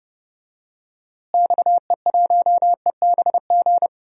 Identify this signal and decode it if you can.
Morse code, heard as a keyed tone, about 31 wpm, 700 Hz. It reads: XE1EBG
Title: DXCC-100 Call Signs - 6 Characters - Repeated 2x - 30wpm